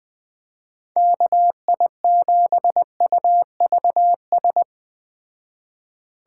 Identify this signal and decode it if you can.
Morse code, heard as a keyed tone, about 20 wpm, 700 Hz. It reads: KI7UVS
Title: Speed Racing - USA 2x3 Call Signs 17wpm